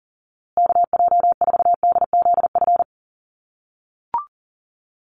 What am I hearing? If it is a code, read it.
Morse code, heard as a keyed tone, about 40 wpm, 700 Hz. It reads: KJ4DZF
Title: USA Call Signs Repeated 2x 40wpm